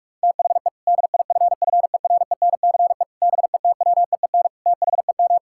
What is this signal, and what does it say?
Morse code, heard as a keyed tone, about 46 wpm, 700 Hz. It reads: THE DIFFERENCE BETWEEN THEM
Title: Sentences from Top 1600 Words 45wpm